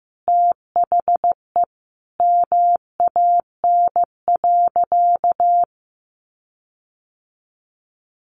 Morse code, heard as a keyed tone, about 15 wpm, 700 Hz. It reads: THE MAN.
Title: Sentences from Top 1500 Words 15wpm